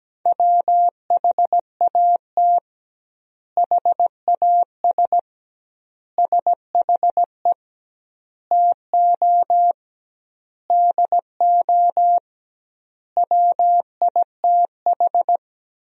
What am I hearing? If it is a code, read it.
Morse code, heard as a keyed tone, about 17 wpm, 700 Hz. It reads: WHAT HAS SHE TO DO WITH
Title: 2x Word-Spacing - Sentences from Top 100 Words - 17wpm